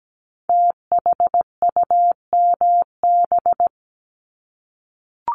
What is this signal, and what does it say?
Morse code, heard as a keyed tone, about 17 wpm, 700 Hz. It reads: THUMB
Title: Multi-letter Phonemes Repeated 3x 17wpm